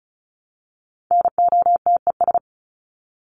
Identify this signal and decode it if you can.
Morse code, heard as a keyed tone, about 34 wpm, 700 Hz. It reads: NOTES
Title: ES:S Suffix - Top 100 Words with Base Words 35wpm